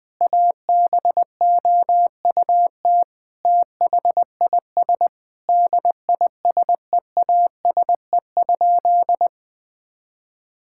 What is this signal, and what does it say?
Morse code, heard as a keyed tone, about 20 wpm, 700 Hz. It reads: ABOUT THIS DISEASE?
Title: Sentences from Top 1000 Words - Review Part 6 - 20wpm